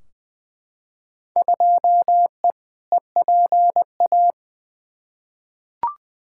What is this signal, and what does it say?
Morse code, heard as a keyed tone, about 20 wpm, 700 Hz. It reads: 2E EPA
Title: Field Day Exchanges Encore - 20wpm